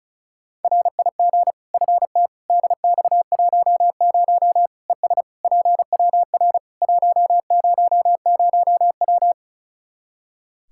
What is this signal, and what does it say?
Morse code, heard as a keyed tone, about 36 wpm, 700 Hz. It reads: RIG FT DX10 ES PWR 100W
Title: K1USN SST QSO Practice 35wpm